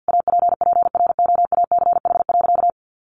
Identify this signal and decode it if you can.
Morse code, heard as a keyed tone, about 50 wpm, 700 Hz. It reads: APPROACH.
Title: Sentences from Top 1400 Words 50wpm